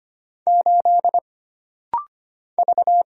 Morse code, heard as a keyed tone, about 25 wpm, 700 Hz. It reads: 8 V
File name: Single Character (Letter:Number:Punctuation:Prosign) - Rapid-Fire - 25wpm